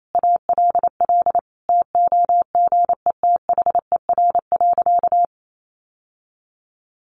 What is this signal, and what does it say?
Morse code, heard as a keyed tone, about 28 wpm, 700 Hz. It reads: ALL TOGETHER.
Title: Sentences from Top 1100 Words 28wpm